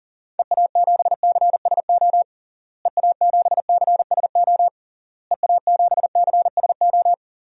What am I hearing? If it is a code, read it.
Morse code, heard as a keyed tone, about 40 wpm, 700 Hz. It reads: EA7CSO EA7CSO EA7CSO
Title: DXCC-100 Call Signs - 6 Characters - Repeated 3x - 40wpm